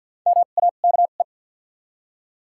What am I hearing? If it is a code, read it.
Morse code, heard as a keyed tone, about 54 wpm, spaced as for 37 wpm, 700 Hz. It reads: MAKE
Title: 2x Character Spacing - Top 100 Words - 50wpm